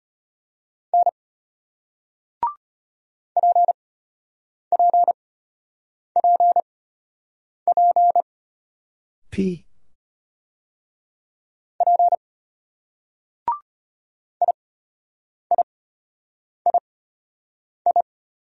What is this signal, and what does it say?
Morse code, heard as a keyed tone, about 33 wpm, spaced as for 8 wpm, 700 Hz. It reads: N PPPP P IIII